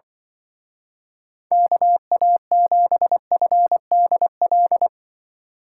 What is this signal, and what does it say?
Morse code, heard as a keyed tone, about 25 wpm, 700 Hz. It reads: KA7FDL
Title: Speed Racing - USA Call Signs 20wpm